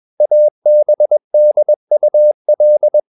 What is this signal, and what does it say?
Morse code, heard as a keyed tone, about 21 wpm, 600 Hz. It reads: ABDUL
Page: CW med Gnister